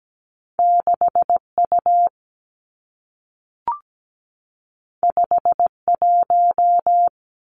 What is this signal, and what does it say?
Morse code, heard as a keyed tone, about 17 wpm, 700 Hz. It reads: 6 U 5 1
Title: Two Challenging Letter-Number Combinations 17wpm